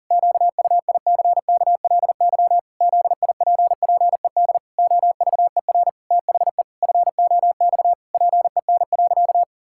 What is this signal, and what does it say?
Morse code, heard as a keyed tone, about 40 wpm, 700 Hz. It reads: QUICKLY ZIPPED OVER THE FOX PEN.
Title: Pangram Challenge v2 - 40wpm